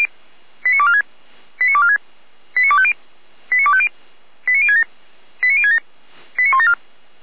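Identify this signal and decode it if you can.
Five-tone selective call (ZVEI), heard as quick runs of five tones, tones of about 80 ms. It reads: 89246 89246 89250 89250 89067 89067 89174
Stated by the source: Five-tone ZVEI selective call received on SDR, from feuerwehr-orlamuende.org (Feuerwehr Orlamünde fire service).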